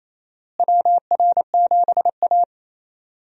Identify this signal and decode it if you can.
Morse code, heard as a keyed tone, about 28 wpm, 700 Hz. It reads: WR7A
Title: USA Call Signs - 4 Characters - Repeated 2x - 28wpm